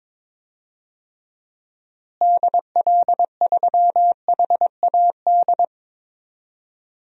Morse code, heard as a keyed tone, about 22 wpm, 700 Hz. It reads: DL3HAD
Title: DXCC-100 Call Signs - 6 Characters - Repeated 2x - 22wpm